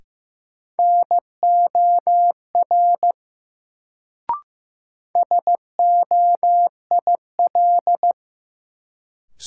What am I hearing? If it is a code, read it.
Morse code, heard as a keyed tone, about 15 wpm, 700 Hz. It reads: NOR SOIL
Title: Sentences from Top 1500 Words - New Word Drill Alt - 15wpm